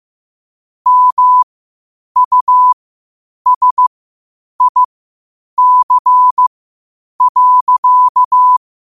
Morse code, heard as a keyed tone, about 15 wpm, spaced as for 8 wpm, 1000 Hz. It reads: MUSIC.
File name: qotdfiles / 8wpm